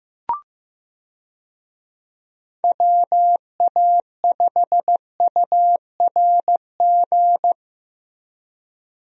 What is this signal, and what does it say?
Morse code, heard as a keyed tone, about 15 wpm, 700 Hz. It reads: WA5URG
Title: USA Call Signs - 6 Characters - Repeated 2x - 15wpm